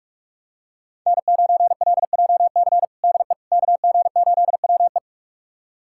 Morse code, heard as a keyed tone, about 42 wpm, 700 Hz. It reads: N9LJC DE KG8WE
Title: USA Callsign DE Callsign - 5 Characters - 45wpm